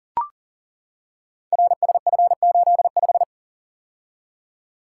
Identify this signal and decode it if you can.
Morse code, heard as a keyed tone, about 38 wpm, 700 Hz. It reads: RSF85
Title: Code Groups - 5 Letter:Numbers - 40wpm